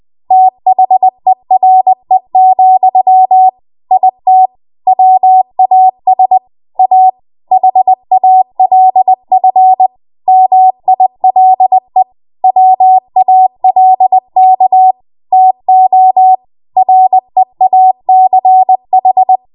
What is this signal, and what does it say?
Morse code, heard as a keyed tone, about 20 wpm, 750 Hz. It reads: THERE, IT WAS A HALF MILE WALK TO REACH
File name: Morse / 20WPM